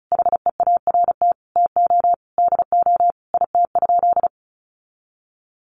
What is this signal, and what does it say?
Morse code, heard as a keyed tone, about 34 wpm, 700 Hz. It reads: HEART TO DO IT?
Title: Sentences from Top 1200 Words 35wpm